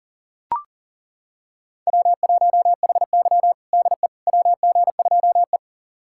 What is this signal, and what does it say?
Morse code, heard as a keyed tone, about 40 wpm, 700 Hz. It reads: W1HY DE WG2E
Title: USA Callsign DE Callsign - 4 Characters - 40wpm